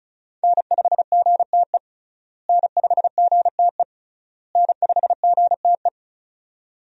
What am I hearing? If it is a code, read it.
Morse code, heard as a keyed tone, about 36 wpm, 700 Hz. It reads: N5GTE N5GTE N5GTE
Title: USA Call Signs - 5 Characters - Repeated 3x - 35wpm